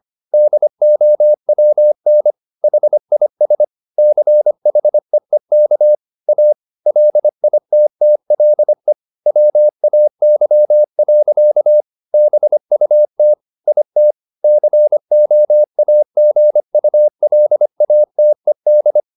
Morse code, heard as a keyed tone, about 25 wpm, 600 Hz. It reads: DOWN HIS CHEEK A LITTLE WAY. BUT IT COAGULATED
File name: the old man and the sea part 1 25wpm